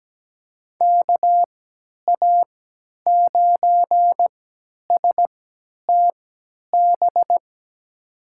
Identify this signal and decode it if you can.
Morse code, heard as a keyed tone, about 17 wpm, spaced as for 10 wpm, 700 Hz. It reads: KA9STB
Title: USA Call Signs - 3x Character Spacing - 17wpm